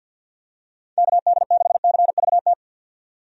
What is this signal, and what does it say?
Morse code, heard as a keyed tone, about 48 wpm, 700 Hz. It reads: KD6CVT
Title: USA Call Signs - 6 Characters - 50wpm